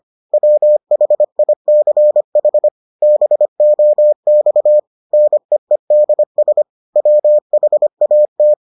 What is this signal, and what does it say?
Morse code, heard as a keyed tone, about 25 wpm, 600 Hz. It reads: WHICH BOX NEEDS WHAT